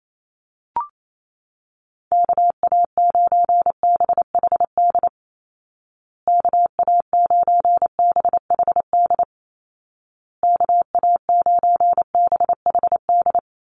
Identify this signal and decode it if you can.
Morse code, heard as a keyed tone, about 28 wpm, 700 Hz. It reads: KA9BHD KA9BHD KA9BHD
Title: Call Signs Repeated 2x - 28wpm